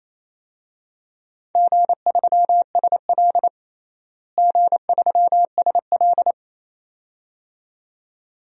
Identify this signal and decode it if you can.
Morse code, heard as a keyed tone, about 28 wpm, 700 Hz. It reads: G3SL G3SL
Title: DXCC-100 Call Signs - 4 Characters - Repeated 2x - 28wpm